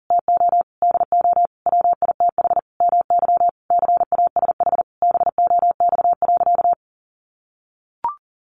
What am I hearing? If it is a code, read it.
Morse code, heard as a keyed tone, about 40 wpm, 700 Hz. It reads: TO DO WITH MY CASH BOX.
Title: Sentences from Top 700 Words 40wpm